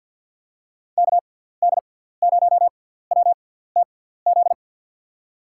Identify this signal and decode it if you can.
Morse code, heard as a keyed tone, about 49 wpm, spaced as for 17 wpm, 700 Hz. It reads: KD0WTZ
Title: USA Call Signs - 6 Characters - 6x Character Spacing - 50wpm